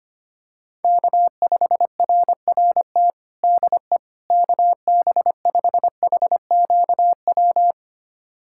Morse code, heard as a keyed tone, about 25 wpm, 700 Hz. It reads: K5RRT DE KB5HQW
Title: USA Callsign DE Callsign 25wpm